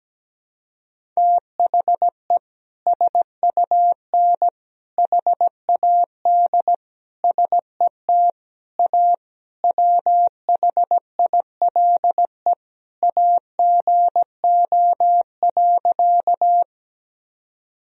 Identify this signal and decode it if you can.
Morse code, heard as a keyed tone, about 17 wpm, 700 Hz. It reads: THE SUN HAD SET A WHILE AGO.
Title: Sentences from Top 1000 Words - Review Part 1 - 17wpm